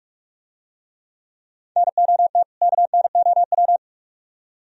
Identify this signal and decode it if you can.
Morse code, heard as a keyed tone, about 45 wpm, 700 Hz. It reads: NOT KNOW